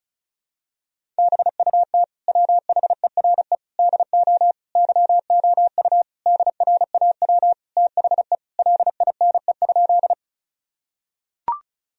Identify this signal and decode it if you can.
Morse code, heard as a keyed tone, about 36 wpm, 700 Hz. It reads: BUT WHERE DO YOU DRAW THE LINE?
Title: Sentences from Top 1000 Words - Review Part 1 - 35wpm